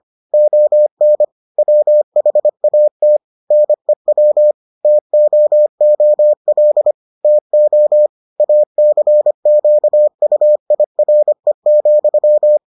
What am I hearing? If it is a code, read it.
Morse code, heard as a keyed tone, about 25 wpm, 600 Hz. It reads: ON WHAT NEW TOOL TO ACQUIRE,